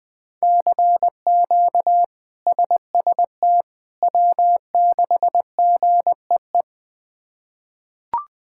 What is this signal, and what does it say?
Morse code, heard as a keyed tone, about 20 wpm, 700 Hz. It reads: CQ SST W6GEE
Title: K1USN SST QSO Practice - 20wpm